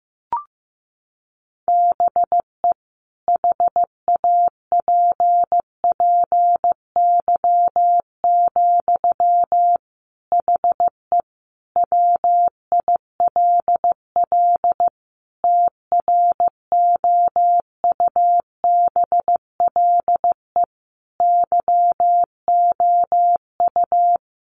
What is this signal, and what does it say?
Morse code, heard as a keyed tone, about 15 wpm, 700 Hz. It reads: BE HAPPY, HE WILL TROUBLE YOU
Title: Sentences from Top 1200 Words 15wpm